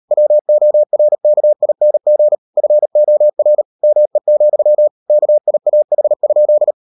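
Morse code, heard as a keyed tone, about 38 wpm, 600 Hz. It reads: WORKING FOR ME, KIAH?
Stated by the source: Morse code practice at 38 WPM, sending Ken Follett's novel Never in English, from CW med Gnister.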